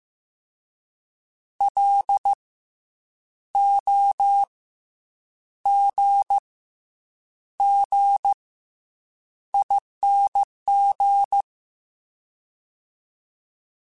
Morse code, heard as a keyed tone, about 15 wpm, 800 Hz. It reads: L O G G ING
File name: Special INGs / Special INGs-mp3s / SlowToFaster ing @13